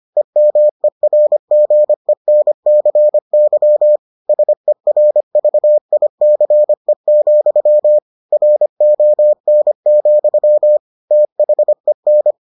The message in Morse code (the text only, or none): EMERGENCY SERVICE, RON, THEN